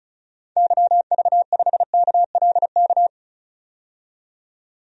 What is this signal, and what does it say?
Morse code, heard as a keyed tone, about 36 wpm, 700 Hz. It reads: YV5KLK
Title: DXCC-100 Call Signs - 6 Characters - 35wpm